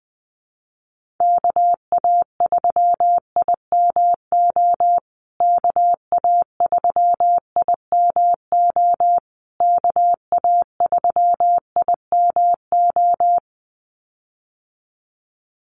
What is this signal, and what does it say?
Morse code, heard as a keyed tone, about 20 wpm, 700 Hz. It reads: KA3IMO KA3IMO KA3IMO
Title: USA Call Signs Repeated 3x 20wpm